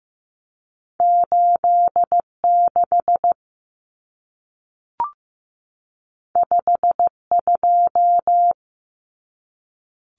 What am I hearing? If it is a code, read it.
Morse code, heard as a keyed tone, about 15 wpm, 700 Hz. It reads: 86 52
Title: Common Cut Numbers - 2-Digit 15wpm